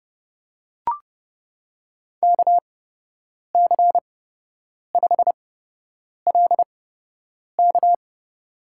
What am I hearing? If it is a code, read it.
Morse code, heard as a keyed tone, about 30 wpm, spaced as for 8 wpm, 700 Hz. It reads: KC5LK
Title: USA Call Signs - 5 Characters - 8x Character Spacing - 30wpm